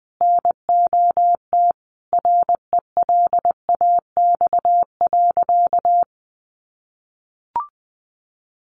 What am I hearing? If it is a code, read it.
Morse code, heard as a keyed tone, about 20 wpm, 700 Hz. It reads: NOT RELAX.
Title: Sentences from Top 1400 Words 20wpm